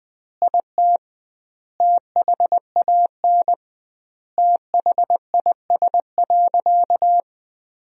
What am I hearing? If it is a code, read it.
Morse code, heard as a keyed tone, about 21 wpm, 700 Hz. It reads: IT THAN THIS.
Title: 2x Word-Spacing - Sentences from Top 100 Words - 20wpm